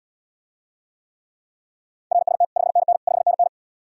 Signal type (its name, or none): Morse code